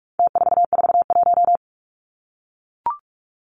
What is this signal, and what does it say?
Morse code, heard as a keyed tone, about 44 wpm, 700 Hz. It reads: T441
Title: Numbers - Contest Style - 4-Digit 45wpm